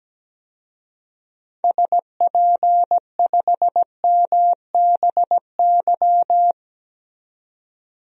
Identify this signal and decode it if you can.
Morse code, heard as a keyed tone, about 17 wpm, 700 Hz. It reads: SP5MBY
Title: DXCC-100 Call Signs - 6 Characters - 17wpm